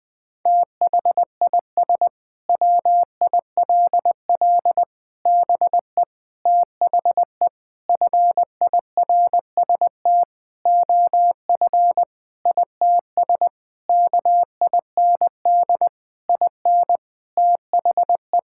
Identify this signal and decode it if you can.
Morse code, heard as a keyed tone, about 20 wpm, 700 Hz. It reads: THIS WILL BE THE FIRST OF ITS KIND IN THE
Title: Sentences with 8-Word Phrases - 20wpm